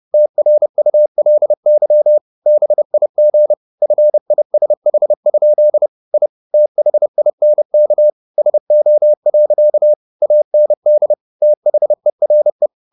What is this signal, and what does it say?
Morse code, heard as a keyed tone, about 30 wpm, 600 Hz. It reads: TRULY BIG FISH? I THINK SO. AND THERE
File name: the old man and the sea / the old man and the sea 1 30wpm